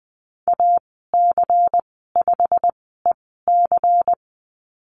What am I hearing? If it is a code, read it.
Morse code, heard as a keyed tone, about 20 wpm, spaced as for 15 wpm, 700 Hz. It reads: AC5EC
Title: 122 Lesson 20g - Number 3 - Call Signs 20wpm